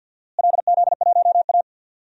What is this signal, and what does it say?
Morse code, heard as a keyed tone, about 45 wpm, 700 Hz. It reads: LZ1A
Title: DXCC-100 Call Signs - 4 Characters - Repeated 3x - 50wpm